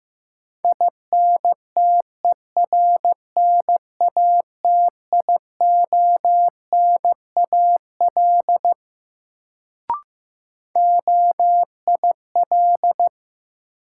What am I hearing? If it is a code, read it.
Morse code, heard as a keyed tone, about 15 wpm, 700 Hz. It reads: INTERNATIONAL OIL
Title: Sentences from Top 1000 Words - Review Part 4 - New Word Drill - 15wpm